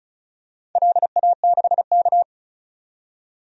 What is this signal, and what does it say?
Morse code, heard as a keyed tone, about 34 wpm, 700 Hz. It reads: LA6K
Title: DXCC-100 Call Signs - 4 Characters - 35wpm